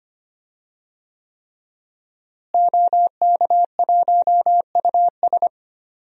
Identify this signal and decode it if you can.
Morse code, heard as a keyed tone, about 25 wpm, 700 Hz. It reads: OK1US
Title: DXCC-100 Call Signs - 5 Characters - Repeated 3x - 25wpm